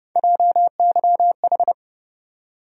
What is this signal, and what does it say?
Morse code, heard as a keyed tone, about 30 wpm, 700 Hz. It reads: JYH
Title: USA Call Sign Postfixes - 3 Letters 30wpm